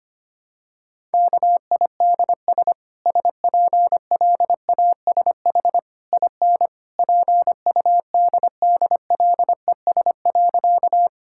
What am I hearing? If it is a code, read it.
Morse code, heard as a keyed tone, about 25 wpm, 700 Hz. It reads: KIDS SPLASH IN PUDDLES.
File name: Easy Sentences with Hints - 25wpm